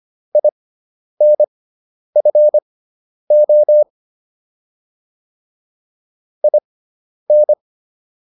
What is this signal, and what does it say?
Morse code, heard as a keyed tone, about 25 wpm, spaced as for 10 wpm, 600 Hz. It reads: INFO IN